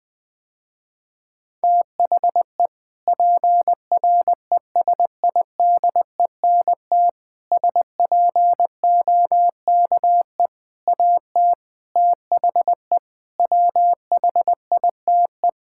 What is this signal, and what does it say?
Morse code, heard as a keyed tone, about 20 wpm, 700 Hz. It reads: THE PRESIDENT SPOKE AT THE WHITE
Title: Sentences with 4-Word Phrases - Spoken Beginning - 20wpm